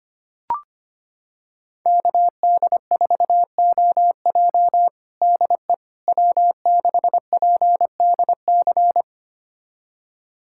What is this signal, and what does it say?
Morse code, heard as a keyed tone, about 25 wpm, 700 Hz. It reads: KD4OJ DE W6PDC
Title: USA Callsign DE Callsign - 5 Characters - 25wpm